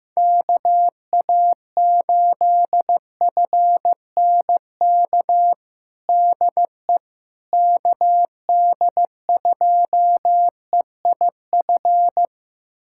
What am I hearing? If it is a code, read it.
Morse code, heard as a keyed tone, about 15 wpm, 700 Hz. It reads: KA8FNK DE KD2EIF
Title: USA Callsign DE Callsign - 6 Characters - 15wpm